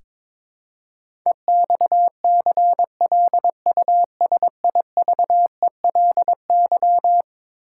Morse code, heard as a keyed tone, about 22 wpm, 700 Hz. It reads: EXCLUSIVELY
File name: practice / 22wpm / Eleven Letter Words - 22wpm